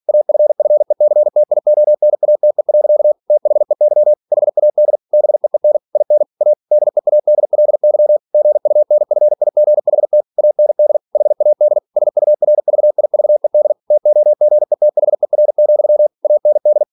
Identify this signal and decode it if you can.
Morse code, heard as a keyed tone, about 46 wpm, 600 Hz. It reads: AFFECTIONATE. THEY HAD BEEN IN A DEADLY GUNFIGHT AND HAD SURVIVED TOGETHER, AND